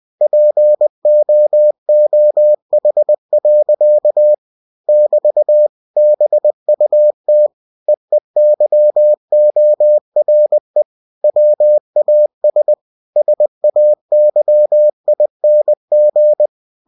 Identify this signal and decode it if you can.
Morse code, heard as a keyed tone, about 20 wpm, 600 Hz. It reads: POOH. <BT> BUT EEYORE WAS SAYING
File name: cw-books / winnie-the-pooh / winnie-20wpm-page-0109